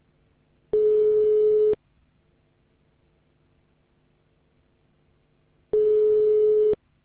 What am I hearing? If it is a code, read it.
Ringback tone (Europe, 425 Hz).